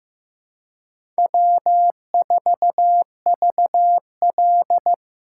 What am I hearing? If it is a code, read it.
Morse code, heard as a keyed tone, about 15 wpm, 700 Hz. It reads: W4VL
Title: USA Call Signs - 4 Characters - 4x Character Spacing - 15wpm